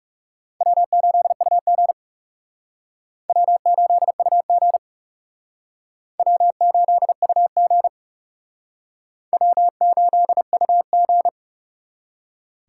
Morse code, heard as a keyed tone, about 37 wpm, 700 Hz. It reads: W8UG W8UG W8UG W8UG
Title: Speed Racing - USA Call Signs 30wpm